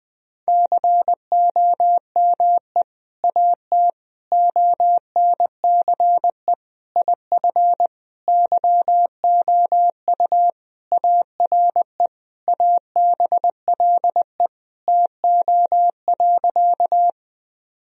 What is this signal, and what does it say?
Morse code, heard as a keyed tone, about 20 wpm, 700 Hz. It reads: COME AT ONCE IF YOU ARE ABLE TO.
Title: Sentences from Top 1000 Words - Review Part 2 - 20wpm